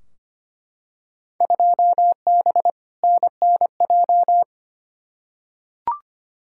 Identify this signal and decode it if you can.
Morse code, heard as a keyed tone, about 25 wpm, 700 Hz. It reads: 2B NNJ
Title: Field Day Exchanges Encore - 25wpm